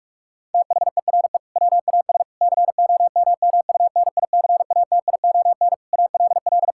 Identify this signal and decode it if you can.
Morse code, heard as a keyed tone, about 46 wpm, 700 Hz. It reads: THERE WAS COMMUNICATION ALL
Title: Sentences from Top 1000 Words - Review Part 6 - 45wpm